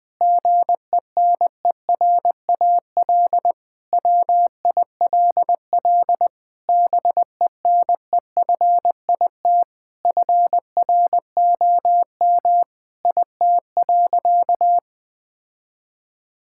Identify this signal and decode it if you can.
Morse code, heard as a keyed tone, about 20 wpm, 700 Hz. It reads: GENERAL WILL BENEFIT FROM IT.
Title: Sentences from Top 1000 Words - Review Part 6 - 20wpm